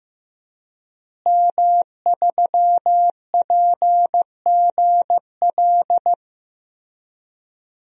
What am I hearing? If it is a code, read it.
Morse code, heard as a keyed tone, about 15 wpm, 700 Hz. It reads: M3PGL
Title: DXCC-100 Call Signs - 5 Characters - Repeated 2x - 15wpm